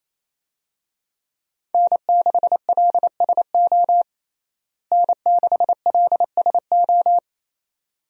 Morse code, heard as a keyed tone, about 27 wpm, 700 Hz. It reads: N6LSO N6LSO
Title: USA Call Signs - 5 Characters - Repeated 2x - 28wpm